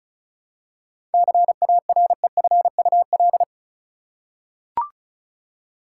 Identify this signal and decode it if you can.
Morse code, heard as a keyed tone, about 34 wpm, 700 Hz. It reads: CAREFUL
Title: Seven Letter Words - 35wpm